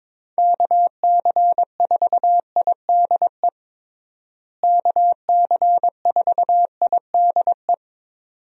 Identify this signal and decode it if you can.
Morse code, heard as a keyed tone, about 22 wpm, 700 Hz. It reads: KC4IDE KC4IDE
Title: USA Call Signs - 6 Characters - Repeated 2x - 22wpm